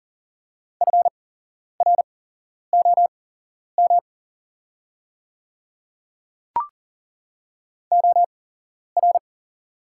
Morse code, heard as a keyed tone, about 40 wpm, spaced as for 11 wpm, 700 Hz. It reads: FROM OR
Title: 8x Character Spacing - Top 100 Words - 40wpm